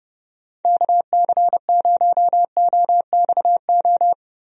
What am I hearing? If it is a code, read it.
Morse code, heard as a keyed tone, about 30 wpm, 700 Hz. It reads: KC0OXO
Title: USA Call Signs - 6 Characters - Repeated 2x - 30wpm